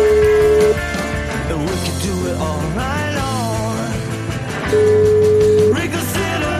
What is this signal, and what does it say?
Ringback tone (Europe, 425 Hz), with noise.